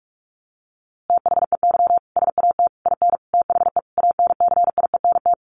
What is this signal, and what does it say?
Morse code, heard as a keyed tone, about 44 wpm, 700 Hz. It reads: THEY SAT IN THE ANCIENT
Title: A Sound Of Thunder 45wpm